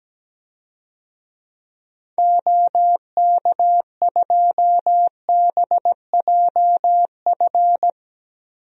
Morse code, heard as a keyed tone, about 17 wpm, 700 Hz. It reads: OK2BJF
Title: DXCC-100 Call Signs - 6 Characters - Repeated 3x - 17wpm